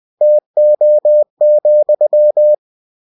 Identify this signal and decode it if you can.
Morse code, heard as a keyed tone, about 20 wpm, 600 Hz. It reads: TO,
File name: cw-books / winnie-the-pooh / winnie-20wpm-page-0039